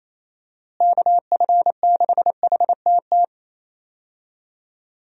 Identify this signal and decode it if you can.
Morse code, heard as a keyed tone, about 28 wpm, 700 Hz. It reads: KF6HTT
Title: USA Call Signs - 6 Characters - Repeated 3x - 28wpm